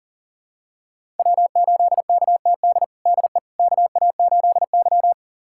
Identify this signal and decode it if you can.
Morse code, heard as a keyed tone, about 40 wpm, 700 Hz. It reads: W8KTD DE KA8Y
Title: USA Callsign DE Callsign 40wpm